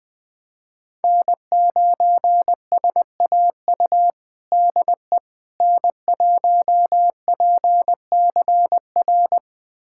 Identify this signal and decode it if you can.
Morse code, heard as a keyed tone, about 20 wpm, 700 Hz. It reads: N9SAU DE N1PCR
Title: USA Callsign DE Callsign - 5 Characters - 20wpm